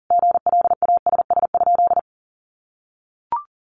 Morse code, heard as a keyed tone, about 40 wpm, 700 Hz. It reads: GLASS?
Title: Sentences from Top 1300 Words 40wpm